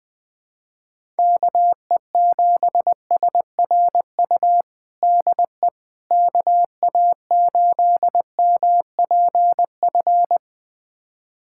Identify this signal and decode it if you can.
Morse code, heard as a keyed tone, about 21 wpm, 700 Hz. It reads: KE7SRU DE KA8MPF
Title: USA Callsign DE Callsign 20wpm